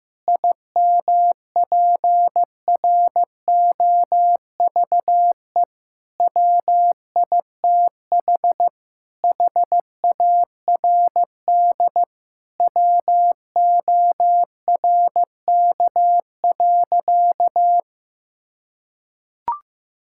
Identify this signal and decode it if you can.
Morse code, heard as a keyed tone, about 15 wpm, 690 Hz. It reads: IMPROVE WITH HARD WORK.
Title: Sentences from Top 1000 Words - Review Part 5 - 15wpm